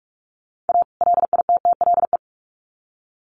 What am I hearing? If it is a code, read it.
Morse code, heard as a keyed tone, about 44 wpm, 700 Hz. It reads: A LITTLE
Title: Sets of 2 Words 45wpm